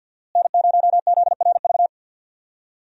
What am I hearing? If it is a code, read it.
Morse code, heard as a keyed tone, about 48 wpm, 700 Hz. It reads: N0ZRV
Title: USA Call Signs - 5 Characters - Repeated 3x - 50wpm